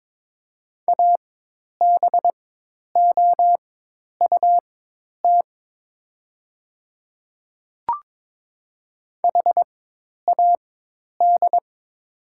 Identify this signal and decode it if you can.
Morse code, heard as a keyed tone, about 22 wpm, spaced as for 10 wpm, 700 Hz. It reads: ABOUT HAD
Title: 4x Character Spacing - Top 300 Words - 22wpm